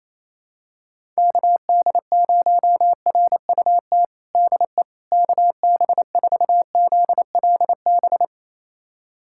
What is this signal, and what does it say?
Morse code, heard as a keyed tone, about 27 wpm, 700 Hz. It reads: KD0RUT DE KB4ZLB
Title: USA Callsign DE Callsign - 6 Characters - 28wpm